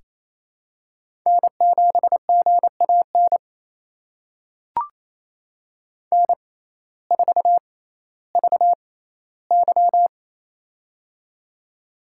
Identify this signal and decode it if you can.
Morse code, heard as a keyed tone, about 28 wpm, spaced as for 15 wpm, 700 Hz. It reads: N7GAN N4VY
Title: USA Call Signs - 6x Character Spacing - 28wpm